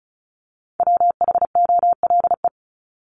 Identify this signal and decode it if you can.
Morse code, heard as a keyed tone, about 34 wpm, 700 Hz. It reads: WHOLE
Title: Top 500 Words 35wpm